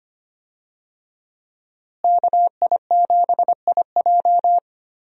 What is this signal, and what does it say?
Morse code, heard as a keyed tone, about 25 wpm, 700 Hz. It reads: KI7IJ
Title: USA Call Signs - 5 Characters - Repeated 3x - 25wpm